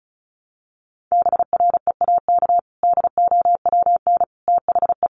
Morse code, heard as a keyed tone, about 34 wpm, 700 Hz. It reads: BREAK DOWN THE
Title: Sentences from Top 900 Words 35wpm